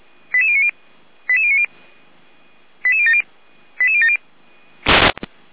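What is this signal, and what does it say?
Five-tone selective call (ZVEI), heard as quick runs of five tones, tones of about 70 ms. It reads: 88090 88090 88070 88070
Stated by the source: ZVEI five-tone selective call received by SDR.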